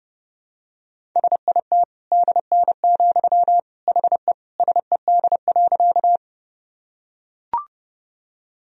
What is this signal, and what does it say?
Morse code, heard as a keyed tone, about 32 wpm, 700 Hz. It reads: SIT DN, HE SED.
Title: Sentences from Top 500 Words and Abbreviations 30wpm